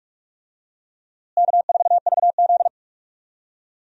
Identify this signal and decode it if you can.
Morse code, heard as a keyed tone, about 41 wpm, 700 Hz. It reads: K4VZ
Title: USA Call Signs - 4 Characters - Repeated 3x - 45wpm